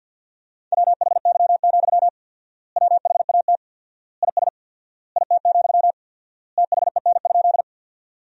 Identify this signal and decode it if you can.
Morse code, heard as a keyed tone, about 53 wpm, 700 Hz. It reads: WHY, WHAT IS IT, THEN?
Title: 4x Word-Spacing - Sentences from Top 200 Words - 50wpm